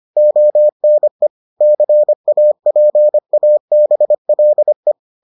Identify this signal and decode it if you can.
Morse code, heard as a keyed tone, about 25 wpm, 600 Hz. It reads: ONE CAPABLE